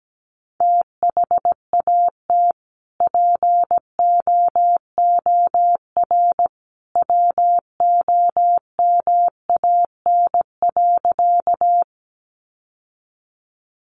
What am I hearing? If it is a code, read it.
Morse code, heard as a keyed tone, about 17 wpm, 700 Hz. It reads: THAT POOR WOMAN.
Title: Sentences from Top 1100 Words 17wpm